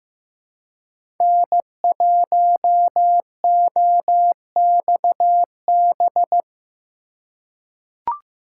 Morse code, heard as a keyed tone, about 15 wpm, 700 Hz. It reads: N1OXB
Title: USA Call Signs - 5 Characters - 4x Character Spacing - 15wpm